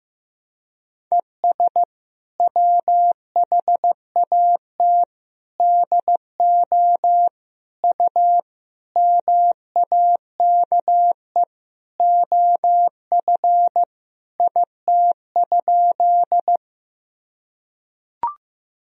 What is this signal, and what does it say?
Morse code, heard as a keyed tone, about 15 wpm, 700 Hz. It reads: ES WHAT DO U MAKE OF IT?
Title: Sentences from Top 100 Words and Abbreviations 15wpm